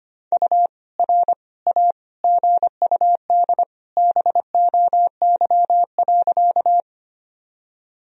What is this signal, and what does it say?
Morse code, heard as a keyed tone, about 25 wpm, 700 Hz. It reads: U R A GUD BOY.
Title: Sentences from Top 200 Words and Abbreviations 25wpm